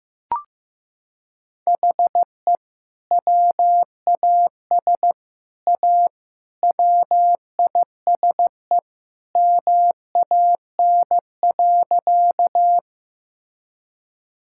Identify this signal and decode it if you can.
Morse code, heard as a keyed tone, about 15 wpm, 700 Hz. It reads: HE WAS A WISE MAN.
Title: Sentences from Top 1600 Words 15wpm